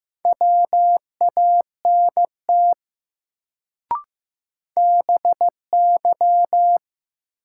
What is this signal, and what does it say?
Morse code, heard as a keyed tone, about 15 wpm, 700 Hz. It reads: WANT BY
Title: Sets of 2 Words - New Word Drill - 15wpm